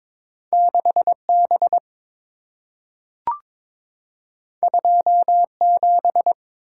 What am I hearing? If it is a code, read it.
Morse code, heard as a keyed tone, about 22 wpm, 690 Hz. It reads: 6B 27
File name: practice / 22wpm / Two Letter-Number Combinations - 22wpm